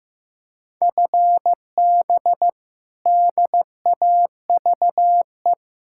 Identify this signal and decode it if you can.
Morse code, heard as a keyed tone, about 15 wpm, 700 Hz. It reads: FB DAVE
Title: K1USN SST QSO Practice 15wpm